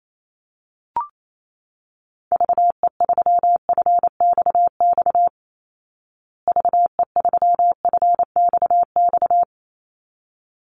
Morse code, heard as a keyed tone, about 28 wpm, 700 Hz. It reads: VE3FXX VE3FXX
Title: Call Signs Repeated 3x - 28wpm